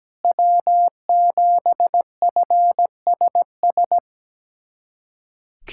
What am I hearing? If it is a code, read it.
Morse code, heard as a keyed tone, about 17 wpm, 700 Hz. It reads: W7FSS
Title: USA Callsign DE Callsign - 5 Characters - 17wpm